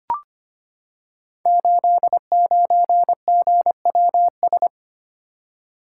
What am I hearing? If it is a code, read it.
Morse code, heard as a keyed tone, about 25 wpm, 700 Hz. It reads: 89GWS
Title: Code Groups - 5 Letter:Numbers - 25wpm